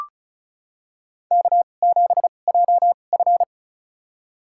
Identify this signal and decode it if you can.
Morse code, heard as a keyed tone, about 34 wpm, spaced as for 25 wpm, 700 Hz. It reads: K7JF
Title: USA Call Signs - 2x Character Spacing - 35wpm